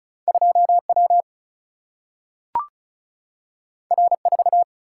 Morse code, heard as a keyed tone, about 36 wpm, 700 Hz. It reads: 2 W R 4
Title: Single-Letter Single-Number 35wpm